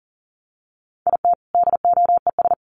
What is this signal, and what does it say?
Morse code, heard as a keyed tone, about 40 wpm, 700 Hz. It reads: IT DOES
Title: Sets of 2 Words 40wpm